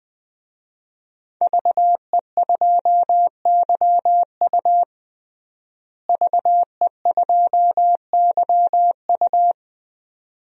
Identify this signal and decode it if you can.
Morse code, heard as a keyed tone, about 21 wpm, 700 Hz. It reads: VE2YU VE2YU
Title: DXCC-100 Call Signs - 5 Characters - Repeated 2x - 20wpm